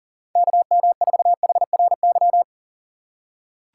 Morse code, heard as a keyed tone, about 38 wpm, 700 Hz. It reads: KM4HRY
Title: USA Call Signs - 6 Characters - 3x Character Spacing - 40wpm